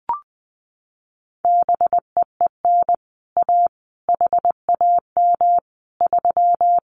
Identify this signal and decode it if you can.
Morse code, heard as a keyed tone, about 20 wpm, 700 Hz. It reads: BEEN A HAM 3
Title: Common QSO Phrases Encore 20wpm